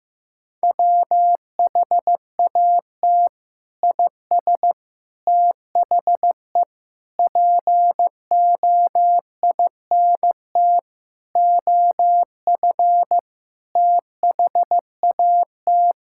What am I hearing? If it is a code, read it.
Morse code, heard as a keyed tone, about 15 wpm, 700 Hz. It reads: WHAT IS THE POINT OF THAT
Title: Sentences from Top 1000 Words - Review Part 5 - 15wpm